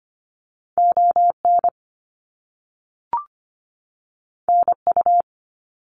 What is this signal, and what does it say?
Morse code, heard as a keyed tone, about 25 wpm, 700 Hz. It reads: ON NU
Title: Canadian Provinces and Territories 25wpm